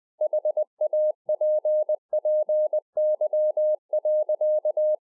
Morse code, heard as a keyed tone, about 20 wpm, 610 Hz. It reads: HAPPY.